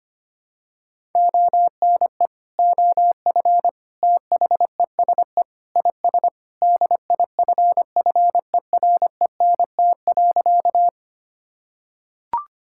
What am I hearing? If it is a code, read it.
Morse code, heard as a keyed tone, about 25 wpm, 700 Hz. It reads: ONE OF THESE IS DIFFERENT.
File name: practice / 25wpm / Sentences with 2-Word Phrases - 25wpm